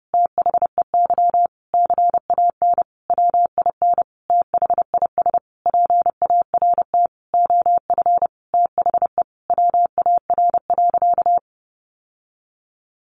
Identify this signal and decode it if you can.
Morse code, heard as a keyed tone, about 30 wpm, 700 Hz. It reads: THEY CAN WIN THIS PART OF THE WAR.
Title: Sentences from Top 1500 Words 30wpm